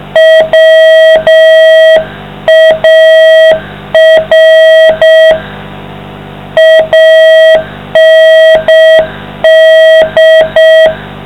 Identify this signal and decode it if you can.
Morse code, heard as a keyed tone, about 6 wpm, 660 Hz, with noise. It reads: WAR AND